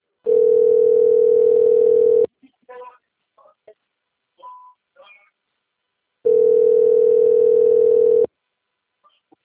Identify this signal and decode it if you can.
Ringback tone (North America), 440 + 480 Hz, with speech.
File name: rg-1600-unknown- - - .8706